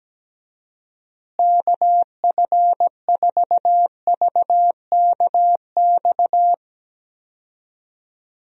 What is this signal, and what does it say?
Morse code, heard as a keyed tone, about 17 wpm, 700 Hz. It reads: KF4VKX
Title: USA Call Signs - 6 Characters - Repeated 2x - 17wpm